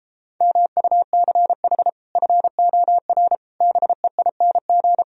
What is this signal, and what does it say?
Morse code, heard as a keyed tone, about 33 wpm, 700 Hz. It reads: MUCH FOR BEING